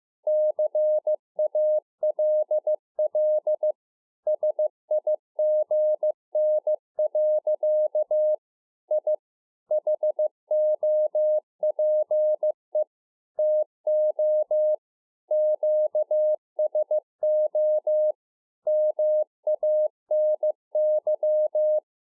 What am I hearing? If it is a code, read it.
Morse code, heard as a keyed tone, about 15 wpm, 610 Hz. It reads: CALL SIGN. I HOPE TO QSO MANY